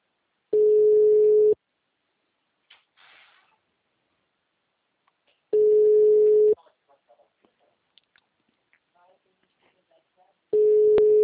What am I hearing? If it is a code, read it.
Ringback tone (Europe, 425 Hz), with speech.